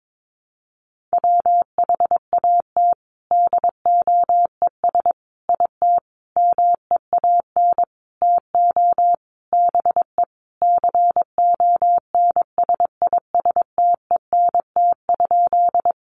Morse code, heard as a keyed tone, about 22 wpm, 700 Hz. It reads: WHAT DOES IT MEAN TO BE CONSISTENT?
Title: Sentences from Top 1500 Words 22wpm